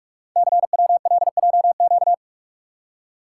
Morse code, heard as a keyed tone, about 44 wpm, 690 Hz. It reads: CWLJQ
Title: Code Groups - 5 Letters - 45wpm